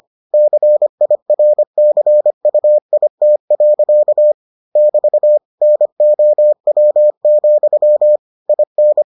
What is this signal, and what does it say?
Morse code, heard as a keyed tone, about 25 wpm, 600 Hz. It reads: CIRCUIT. <BT> NOW, IN